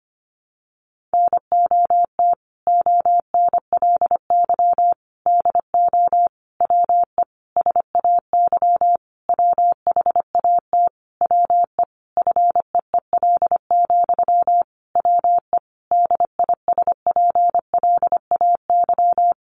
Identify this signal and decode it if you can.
Morse code, heard as a keyed tone, about 25 wpm, 700 Hz. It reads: NOT ONLY DO WE SAY WHAT WE FEEL, WE DISPLAY
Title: Sentences from Top 1200 Words 25wpm